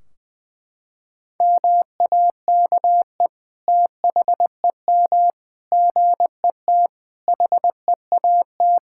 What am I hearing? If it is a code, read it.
Morse code, heard as a keyed tone, about 21 wpm, 700 Hz. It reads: MAKE THEM GET HEAT
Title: Sentences from Top 1600 Words 20wpm